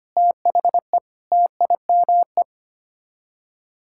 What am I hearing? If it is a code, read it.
Morse code, heard as a keyed tone, about 25 wpm, 700 Hz. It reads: THE TIME
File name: Sets of 9 Words - 25wpm